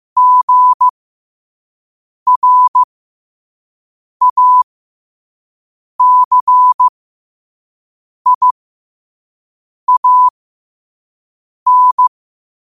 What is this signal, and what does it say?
Morse code, heard as a keyed tone, about 15 wpm, spaced as for 5 wpm, 1000 Hz. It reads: GRACIAN
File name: qotdfiles / 5wpm